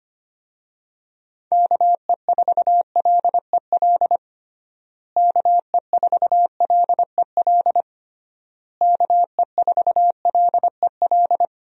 Morse code, heard as a keyed tone, about 25 wpm, 700 Hz. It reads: KE4LEL KE4LEL KE4LEL
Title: USA Call Signs - 6 Characters - Repeated 3x - 25wpm